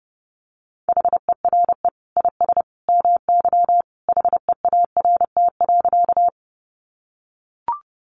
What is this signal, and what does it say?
Morse code, heard as a keyed tone, about 30 wpm, 700 Hz. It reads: HERE IS MY HEART.
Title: Sentences from Top 1200 Words 30wpm